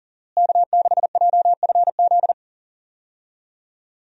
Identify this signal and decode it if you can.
Morse code, heard as a keyed tone, about 40 wpm, 700 Hz. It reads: K6JFZ
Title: USA Call Signs - 5 Characters - Repeated 2x - 40wpm